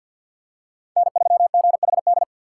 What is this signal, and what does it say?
Morse code, heard as a keyed tone, about 47 wpm, 700 Hz. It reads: N3GHD
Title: USA Call Signs - 6x Character Spacing - 50wpm